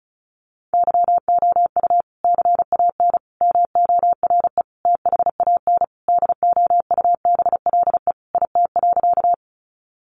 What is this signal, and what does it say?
Morse code, heard as a keyed tone, about 34 wpm, 700 Hz. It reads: YOU CAN MORE THAN DOUBLE IT.
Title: Sentences from Top 1400 Words 35wpm